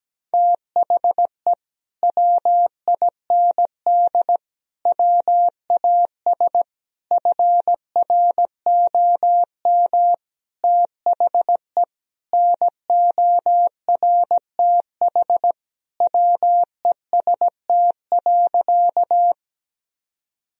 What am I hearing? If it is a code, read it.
Morse code, heard as a keyed tone, about 17 wpm, 700 Hz. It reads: THE WIND WAS FROM THE NORTH WEST.
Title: Sentences from Top 1000 Words - Review Part 2 - 17wpm